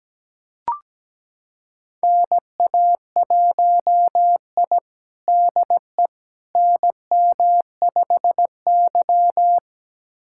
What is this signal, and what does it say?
Morse code, heard as a keyed tone, about 17 wpm, 700 Hz. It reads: NA1I DE NM5Y
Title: USA Callsign DE Callsign - 4 Characters - 17wpm